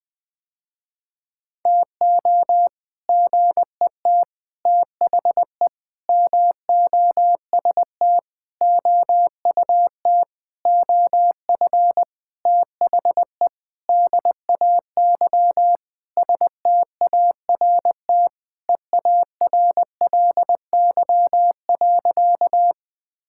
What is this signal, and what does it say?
Morse code, heard as a keyed tone, about 19 wpm, 700 Hz. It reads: TO GET THE MOST OUT OF THE DAY START EARLY.
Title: Sentences with 7-Word Phrases - Spoken Beginning - 20wpm